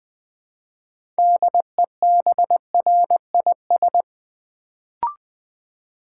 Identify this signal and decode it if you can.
Morse code, heard as a keyed tone, about 20 wpm, 700 Hz. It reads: DEBRIS
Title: Build A Word - 3 Letters At A Time - 6-Letter Words - 20wpm